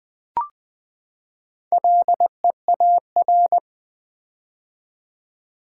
Morse code, heard as a keyed tone, about 20 wpm, 700 Hz. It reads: LEAR
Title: Build A Word - 4 Letters At A Time - 8-Letter Words - 20wpm